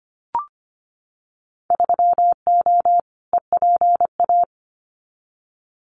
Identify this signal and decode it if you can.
Morse code, heard as a keyed tone, about 25 wpm, 700 Hz. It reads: 3O EPA
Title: Winter Field Day Exchanges Encore 25wpm